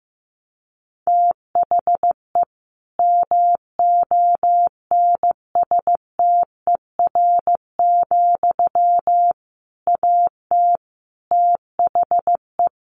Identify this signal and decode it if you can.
Morse code, heard as a keyed tone, about 15 wpm, 700 Hz. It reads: THE MONSTER, AT THE
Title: A Sound Of Thunder 15wpm